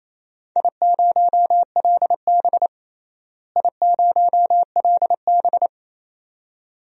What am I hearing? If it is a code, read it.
Morse code, heard as a keyed tone, about 28 wpm, 700 Hz. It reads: I0LB I0LB
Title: DXCC-100 Call Signs - 4 Characters - Repeated 2x - 28wpm